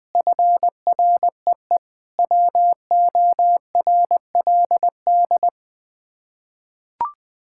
Morse code, heard as a keyed tone, about 20 wpm, 700 Hz. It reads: FREE WORLD
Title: Sets of 6 Words - 20wpm